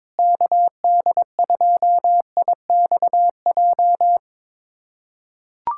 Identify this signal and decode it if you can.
Morse code, heard as a keyed tone, about 21 wpm, 700 Hz. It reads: KD2IXJ
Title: USA Callsign DE Callsign - 6 Characters - 22wpm